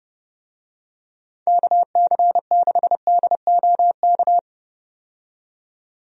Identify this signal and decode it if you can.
Morse code, heard as a keyed tone, about 30 wpm, 700 Hz. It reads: KC6DOK
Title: USA Call Signs - 6 Characters - 30wpm